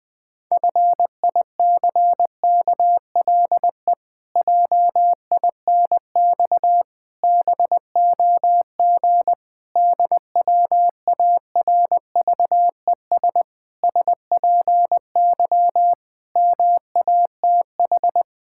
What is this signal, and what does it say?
Morse code, heard as a keyed tone, about 20 wpm, 700 Hz. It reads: FICKLE JINX BOG DWARVES SPY MATH
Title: Pangram Challenge v2 - 20wpm